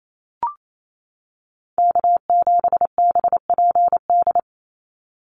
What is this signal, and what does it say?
Morse code, heard as a keyed tone, about 28 wpm, 700 Hz. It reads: K7BPD
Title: USA Call Signs 28wpm